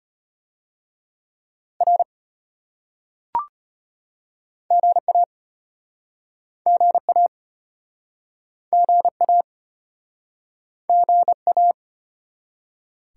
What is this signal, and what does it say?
Morse code, heard as a keyed tone, about 33 wpm, 700 Hz. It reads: R GA GA GA GA